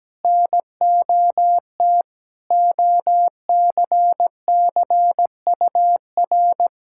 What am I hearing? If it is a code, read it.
Morse code, heard as a keyed tone, about 17 wpm, 700 Hz. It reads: NOT OCCUR
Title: Sentences from Top 1000 Words - Review Part 6 - 17wpm